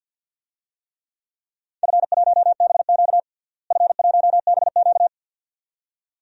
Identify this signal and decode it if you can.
Morse code, heard as a keyed tone, about 52 wpm, 700 Hz. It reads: F1BQ F1BQ
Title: DXCC-100 Call Signs - 4 Characters - Repeated 2x - 50wpm